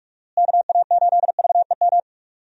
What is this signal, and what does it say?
Morse code, heard as a keyed tone, about 43 wpm, 700 Hz. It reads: KA8VEM
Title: USA Call Signs - 6 Characters - Repeated 2x - 45wpm